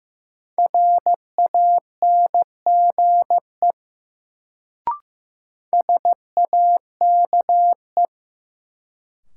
Morse code, heard as a keyed tone, about 15 wpm, 700 Hz. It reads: RANGE SAKE
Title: Sets of 4 Words - New Word Drill - 15wpm